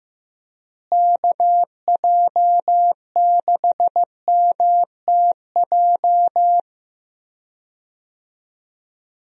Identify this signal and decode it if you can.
Morse code, heard as a keyed tone, about 15 wpm, 700 Hz. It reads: KJ6MTJ
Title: USA Call Signs - 6 Characters - Repeated 3x - 15wpm